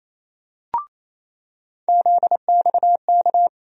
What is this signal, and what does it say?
Morse code, heard as a keyed tone, about 28 wpm, 700 Hz. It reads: ZXK
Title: USA Call Sign Postfixes - 3 Letters 28wpm